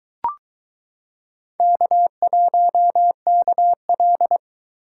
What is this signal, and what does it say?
Morse code, heard as a keyed tone, about 22 wpm, 700 Hz. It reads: K1KL